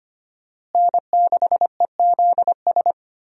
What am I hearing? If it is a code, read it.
Morse code, heard as a keyed tone, about 25 wpm, 700 Hz. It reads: N6EZS
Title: USA Call Signs - 5 Characters - Repeated 2x - 25wpm